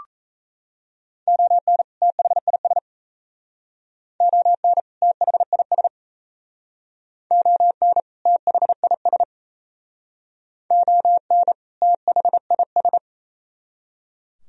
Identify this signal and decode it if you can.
Morse code, heard as a keyed tone, about 38 wpm, 700 Hz. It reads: ON THIS ON THIS ON THIS ON THIS